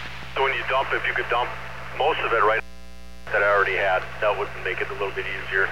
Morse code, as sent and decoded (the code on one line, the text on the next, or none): none